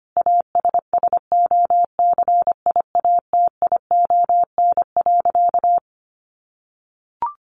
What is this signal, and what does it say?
Morse code, heard as a keyed tone, about 25 wpm, 700 Hz. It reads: ASSOCIATION.
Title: Sentences from Top 1200 Words 25wpm